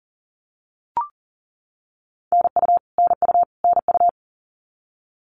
Morse code, heard as a keyed tone, about 40 wpm, 700 Hz. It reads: NU NU NU
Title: Canadian Provinces and Territories - Repeated 3x - 40wpm